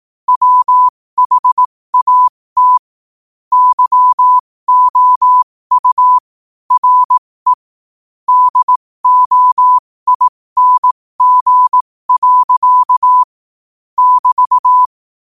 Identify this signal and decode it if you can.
Morse code, heard as a keyed tone, about 18 wpm, spaced as for 15 wpm, 1000 Hz. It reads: WHAT YOU RE DOING. <BT>